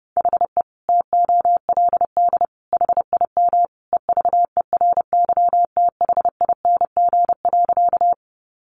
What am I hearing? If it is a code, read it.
Morse code, heard as a keyed tone, about 30 wpm, 700 Hz. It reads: HE TOLD HIM EVERYTHING.
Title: Sentences from Top 700 Words 30wpm